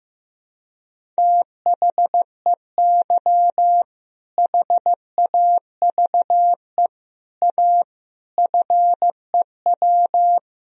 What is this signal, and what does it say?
Morse code, heard as a keyed tone, about 16 wpm, 700 Hz. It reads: THEY HAVE A FEW
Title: Sentences from Top 1000 Words - Review Part 1 - 15wpm